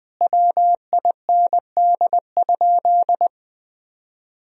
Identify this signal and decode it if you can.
Morse code, heard as a keyed tone, about 20 wpm, 700 Hz. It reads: WIND?
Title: Sentences from Top 1000 Words - Review Part 2 - 20wpm